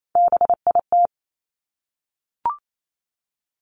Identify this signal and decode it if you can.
Morse code, heard as a keyed tone, about 28 wpm, 700 Hz. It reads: BIT
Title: Top 751-1000 Words 28wpm